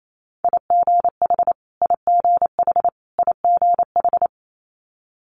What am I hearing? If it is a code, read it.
Morse code, heard as a keyed tone, about 28 wpm, 700 Hz. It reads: IGH IGH IGH
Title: Multi-letter Phonemes Repeated 3x 28wpm